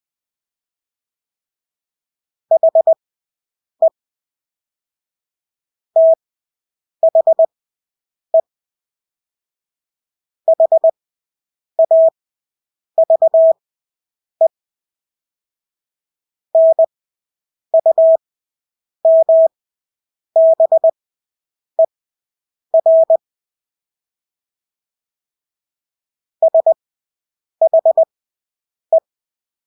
Morse code, heard as a keyed tone, about 20 wpm, spaced as for 8 wpm, 650 Hz. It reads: HE THE HAVE NUMBER SHE
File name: Engelske ord 20/8 wpm